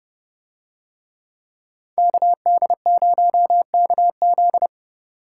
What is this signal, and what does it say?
Morse code, heard as a keyed tone, about 29 wpm, 700 Hz. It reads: KD0KZ